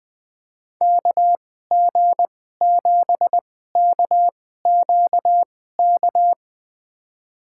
Morse code, heard as a keyed tone, about 21 wpm, spaced as for 15 wpm, 700 Hz. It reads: KG7KQK
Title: USA Call Signs - 6 Characters - 2x Character Spacing - 20wpm